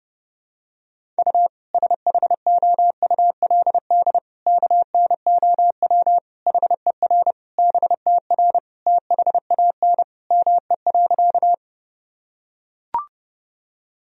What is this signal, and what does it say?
Morse code, heard as a keyed tone, about 32 wpm, 700 Hz. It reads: U SHOULD KNOW HER BTR THAN ME.
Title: Sentences from Top 400 Words and Abbreviations 30wpm